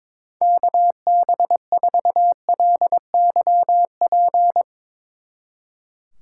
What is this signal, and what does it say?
Morse code, heard as a keyed tone, about 22 wpm, 700 Hz. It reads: KB4LYP
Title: Speed Racing - USA 2x3 Call Signs 22wpm